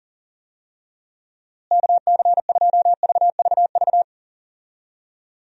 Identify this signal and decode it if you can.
Morse code, heard as a keyed tone, about 40 wpm, 700 Hz. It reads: KC2VVV
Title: USA Call Signs - 6 Characters - Repeated 2x - 40wpm